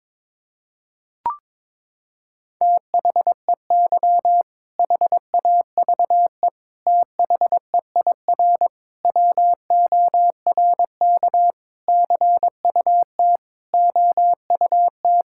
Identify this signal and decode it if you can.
Morse code, heard as a keyed tone, about 22 wpm, 700 Hz. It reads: THEY HAVE THEIR WORK CUT OUT
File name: Sentences with 7-Word Phrases - 22wpm